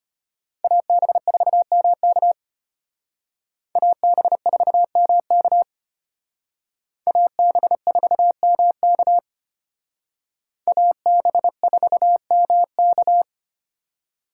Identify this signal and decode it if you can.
Morse code, heard as a keyed tone, about 32 wpm, 700 Hz. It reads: AB4MK AB4MK AB4MK AB4MK